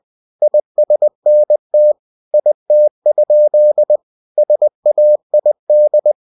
Morse code, heard as a keyed tone, about 20 wpm, 600 Hz. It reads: ISNT IT? SAID
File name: cw-books / winnie-the-pooh / winnie-20wpm-page-0031